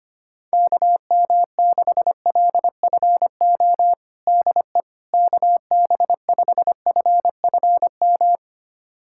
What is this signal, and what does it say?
Morse code, heard as a keyed tone, about 25 wpm, 700 Hz. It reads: KM6LFO DE KB5FFM
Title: USA Callsign DE Callsign - 6 Characters - 25wpm